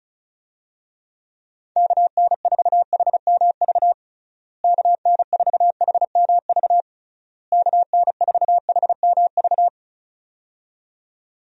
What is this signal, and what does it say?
Morse code, heard as a keyed tone, about 36 wpm, 700 Hz. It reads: KN4HMV KN4HMV KN4HMV
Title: USA Call Signs - 6 Characters - Repeated 3x - 35wpm